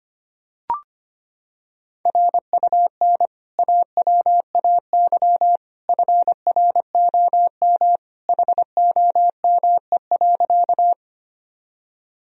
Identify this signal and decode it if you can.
Morse code, heard as a keyed tone, about 25 wpm, 700 Hz. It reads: RUN AWAY FROM HOME.
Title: Sentences from Top 1000 Words - Review Part 1 - 25wpm